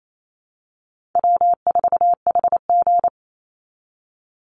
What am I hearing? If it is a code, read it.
Morse code, heard as a keyed tone, about 28 wpm, 700 Hz. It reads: W4HG
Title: Call Signs Repeated 2x - 28wpm